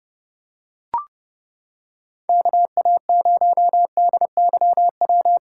Morse code, heard as a keyed tone, about 28 wpm, 700 Hz. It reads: KA0DYW